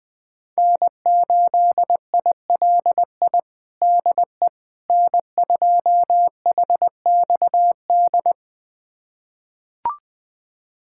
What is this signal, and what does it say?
Morse code, heard as a keyed tone, about 20 wpm, 700 Hz. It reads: N8ILI DE N2HXD
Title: USA Callsign DE Callsign - 5 Characters - 20wpm